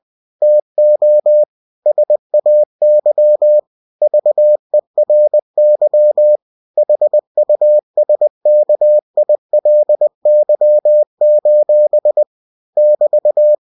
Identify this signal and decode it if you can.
Morse code, heard as a keyed tone, about 20 wpm, 600 Hz. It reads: TO SAY VERY HUSKILY: <BT>
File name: cw-books / winnie-the-pooh / winnie-20wpm-page-0064